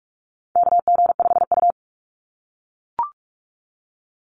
Morse code, heard as a keyed tone, about 44 wpm, 700 Hz. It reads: KG5U
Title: Call Signs - 45wpm